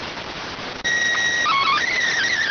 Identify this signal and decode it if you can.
SSTV (slow-scan television), PD 290.